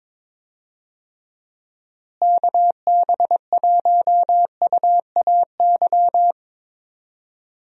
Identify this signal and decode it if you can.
Morse code, heard as a keyed tone, about 22 wpm, 700 Hz. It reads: KB1UAY
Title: USA Call Signs - 6 Characters - Repeated 2x - 22wpm